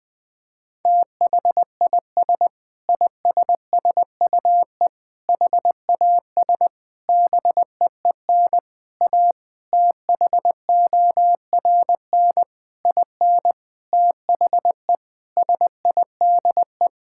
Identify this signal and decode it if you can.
Morse code, heard as a keyed tone, about 20 wpm, 700 Hz. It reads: THIS ISSUE HAS BEEN A THORN IN THE SIDE
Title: Sentences with 8-Word Phrases - 20wpm